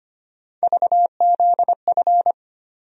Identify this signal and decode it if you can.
Morse code, heard as a keyed tone, about 25 wpm, 700 Hz. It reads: VZF
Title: USA Call Sign Postfixes - 3 Letters 25wpm